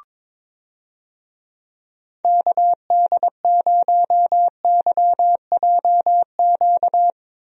Morse code, heard as a keyed tone, about 22 wpm, 700 Hz. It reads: KD0YJQ
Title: USA Call Signs - 6 Characters - Repeated 3x - 22wpm